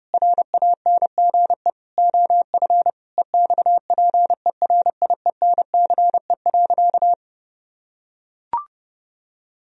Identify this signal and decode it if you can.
Morse code, heard as a keyed tone, about 30 wpm, 700 Hz. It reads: RANGE OF EXPERIENCE.
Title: Sentences from Top 1000 Words - Review Part 4 - 30wpm